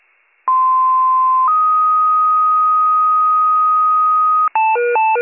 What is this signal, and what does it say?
Two-tone page, heard as a one-second tone, then a three-second tone, about 1040 then 1250 Hz. It is heard from 0.5 to 4.5 s.